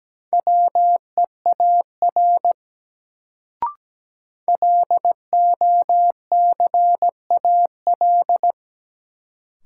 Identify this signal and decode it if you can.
Morse code, heard as a keyed tone, about 17 wpm, 700 Hz. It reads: WEAR LOCAL
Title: Sentences from Top 1500 Words - New Word Drill Alt - 17wpm